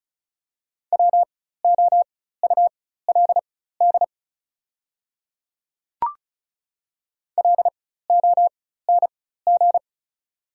Morse code, heard as a keyed tone, about 36 wpm, spaced as for 16 wpm, 700 Hz. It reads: WOULD LONG
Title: 4x Character Spacing - Top 100 Words - 35wpm